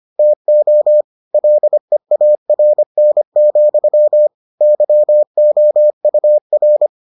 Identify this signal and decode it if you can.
Morse code, heard as a keyed tone, about 25 wpm, 600 Hz. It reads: TO LEARN, YOUR